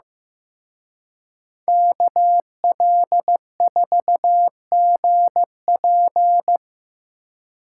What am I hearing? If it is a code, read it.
Morse code, heard as a keyed tone, about 18 wpm, 700 Hz. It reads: KL4GP
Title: Speed Racing - USA 2x2 Call Signs 15wpm